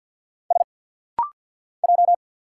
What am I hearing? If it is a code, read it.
Morse code, heard as a keyed tone, about 49 wpm, 700 Hz. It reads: S J
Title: Single Letter-Number Alt - Rapid-Fire 50wpm